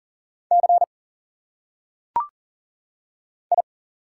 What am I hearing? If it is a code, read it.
Morse code, heard as a keyed tone, about 41 wpm, 700 Hz. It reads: C I
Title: Single Letters - Phonetic Pronunciation 40wpm